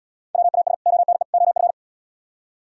Morse code, heard as a keyed tone, about 70 wpm, 700 Hz. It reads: WAS ONE OF